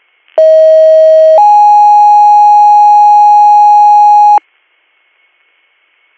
Two-tone page, heard as a one-second tone, then a three-second tone, about 640 then 830 Hz. It is heard from 0.4 to 4.4 s.